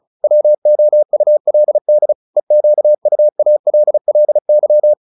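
Morse code, heard as a keyed tone, about 34 wpm, 600 Hz. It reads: WOULD EQUALLY
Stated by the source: Morse practice at 35 wpm from CW med Gnister, sending Ken Follett's novel Never in English.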